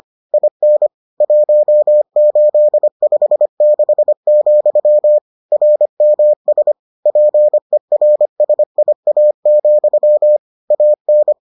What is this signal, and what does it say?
Morse code, heard as a keyed tone, about 25 wpm, 600 Hz. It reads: IN 1856, RMS PERSIA, AN